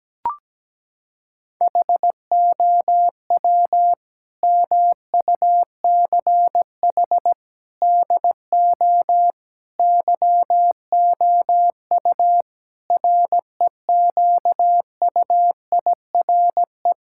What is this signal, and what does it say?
Morse code, heard as a keyed tone, about 17 wpm, 700 Hz. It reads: HOW MUCH DO YOU REQUIRE
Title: Sentences from Top 1000 Words - Review Part 6 - 17wpm